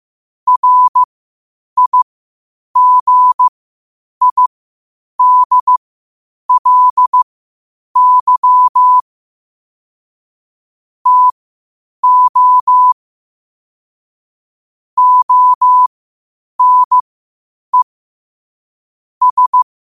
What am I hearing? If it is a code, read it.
Morse code, heard as a keyed tone, about 15 wpm, spaced as for 8 wpm, 1000 Hz. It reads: RIGIDLY TO ONE S
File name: qotdfiles / 8wpm